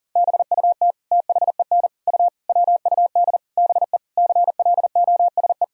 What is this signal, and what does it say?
Morse code, heard as a keyed tone, about 40 wpm, 700 Hz. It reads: BUT THEN U WUD BE CLOSE
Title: Sentences from Top 300 Words and Abbreviations 40wpm